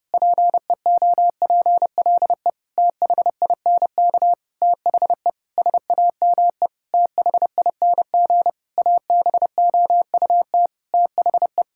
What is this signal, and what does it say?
Morse code, heard as a keyed tone, about 30 wpm, 700 Hz. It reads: PEOPLE THINK THE SAME THING ABOUT THE
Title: Sentences from Top 1000 Words - Review Part 5 - 30wpm